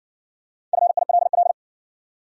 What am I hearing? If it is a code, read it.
Morse code, heard as a keyed tone, about 60 wpm, 700 Hz. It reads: FILL